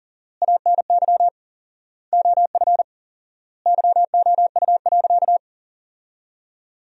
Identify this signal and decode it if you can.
Morse code, heard as a keyed tone, about 40 wpm, 700 Hz. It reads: ANY OF YOU.
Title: 4x Word-Spacing - Sentences from Top 400 Words - 40wpm